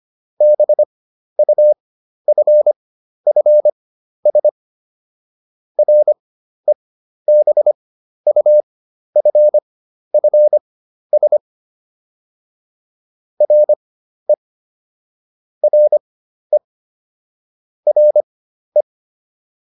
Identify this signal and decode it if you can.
Morse code, heard as a keyed tone, about 25 wpm, spaced as for 12 wpm, 600 Hz. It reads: BUFFS REBUFFS RE RE RE